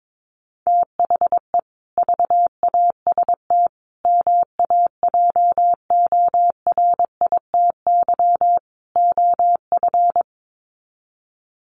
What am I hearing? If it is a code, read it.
Morse code, heard as a keyed tone, about 22 wpm, 700 Hz. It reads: THE VAST MAJORITY OF
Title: Sets of 4 Words 22wpm